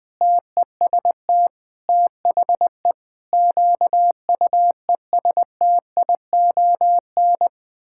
Morse code, heard as a keyed tone, about 20 wpm, 700 Hz. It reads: TEST THE QUESTION
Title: Sentences from Top 1600 Words 20wpm